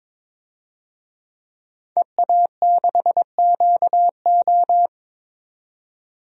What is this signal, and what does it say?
Morse code, heard as a keyed tone, about 22 wpm, 700 Hz. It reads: EA6QO
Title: DXCC-100 Call Signs - 5 Characters - 22wpm